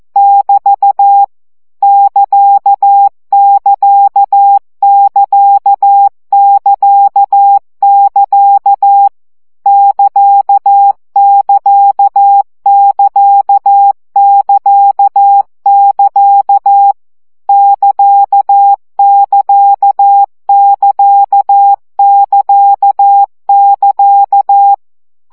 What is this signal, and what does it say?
Morse code, heard as a keyed tone, about 14 wpm, 800 Hz. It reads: <BT> <KA><KA><KA><KA><KA> <KA><KA><KA><KA><KA> <KA><KA><KA><KA><KA>
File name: CW gyakorlas ct 12 baud